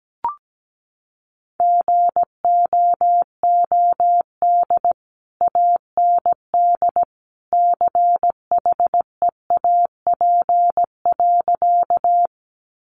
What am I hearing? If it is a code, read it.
Morse code, heard as a keyed tone, about 17 wpm, 700 Hz. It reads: GOOD AND CHEAP.
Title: Sentences from Top 1200 Words 17wpm